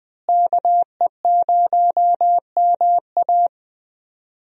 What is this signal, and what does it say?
Morse code, heard as a keyed tone, about 21 wpm, 700 Hz. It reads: KE0MA
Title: USA Call Signs - 5 Characters - Repeated 3x - 20wpm